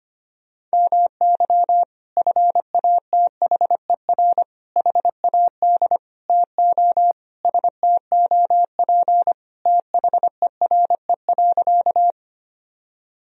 Morse code, heard as a keyed tone, about 25 wpm, 700 Hz. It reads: MY FATHER HAD TO STOP THERE.
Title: Sentences from Top 1000 Words - Review Part 1 - 25wpm